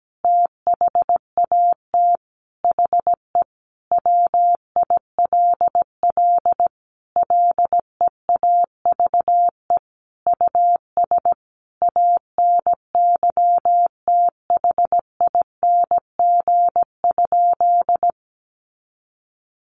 Morse code, heard as a keyed tone, about 17 wpm, 700 Hz. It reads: THAT HE WILL LEAVE US ANYTHING?
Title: Sentences from Top 1100 Words 17wpm